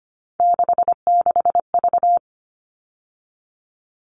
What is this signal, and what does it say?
Morse code, heard as a keyed tone, about 25 wpm, 700 Hz. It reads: 66V
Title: All Cut Numbers - 3-Digit 25wpm